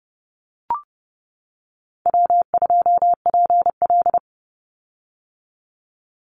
Morse code, heard as a keyed tone, about 30 wpm, 700 Hz. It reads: W2PL
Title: Easy Call Signs - 30wpm